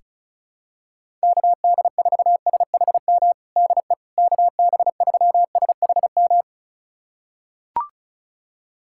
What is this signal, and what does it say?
Morse code, heard as a keyed tone, about 36 wpm, 700 Hz. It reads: KD4SHM DE KB3SHM
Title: USA Callsign DE Callsign - 6 Characters - 35wpm